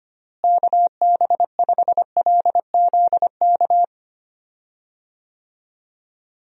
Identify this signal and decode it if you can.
Morse code, heard as a keyed tone, about 25 wpm, 700 Hz. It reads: KB5LZK
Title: USA Call Signs - 6 Characters - Repeated 2x - 25wpm